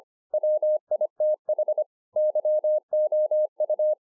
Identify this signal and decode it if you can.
Morse code, heard as a keyed tone, about 25 wpm, 610 Hz. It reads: WITH YOU